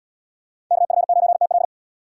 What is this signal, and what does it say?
Morse code, heard as a keyed tone, about 72 wpm, 700 Hz. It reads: KF9TL